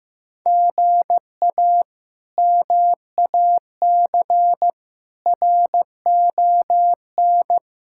Morse code, heard as a keyed tone, about 15 wpm, 700 Hz. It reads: GA MAC RON
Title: K1USN SST QSO Practice - 15wpm